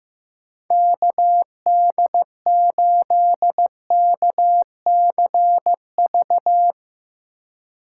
Morse code, heard as a keyed tone, about 15 wpm, 700 Hz. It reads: KD8KCV
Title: USA Call Signs - 4x Character Spacing - 15wpm